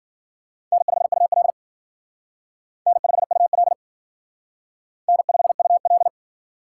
Morse code, heard as a keyed tone, about 50 wpm, 700 Hz. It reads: N5UL N5UL N5UL
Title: Speed Racing - USA 1x2 Call Signs 40wpm